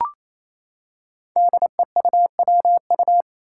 Morse code, heard as a keyed tone, about 28 wpm, 700 Hz. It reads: DEUWU
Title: Five Challenging Letter-Number Combinations 28wpm